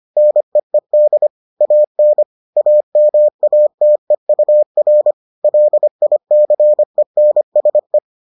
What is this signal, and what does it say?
Morse code, heard as a keyed tone, about 25 wpm, 600 Hz. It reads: NEED AN AMATEUR LICENSE